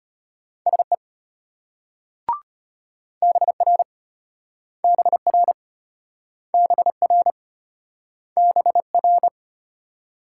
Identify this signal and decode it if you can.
Morse code, heard as a keyed tone, about 34 wpm, 700 Hz. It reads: SE BR BR BR BR